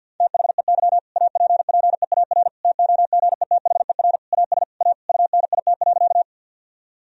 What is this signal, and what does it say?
Morse code, heard as a keyed tone, about 50 wpm, 700 Hz. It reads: THEY APPEAR TOGETHER AS A UNIT.
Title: Sentences from Top 1000 Words - Review Part 2 - 50wpm